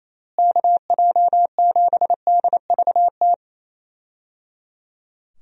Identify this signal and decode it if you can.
Morse code, heard as a keyed tone, about 28 wpm, 700 Hz. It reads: KJ7DVT
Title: USA Call Signs - 6 Characters - Repeated 3x - 28wpm